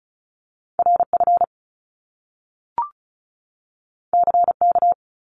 Morse code, heard as a keyed tone, about 36 wpm, 700 Hz. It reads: RF CK
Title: Top Two Letter Combinations in Words 35wpm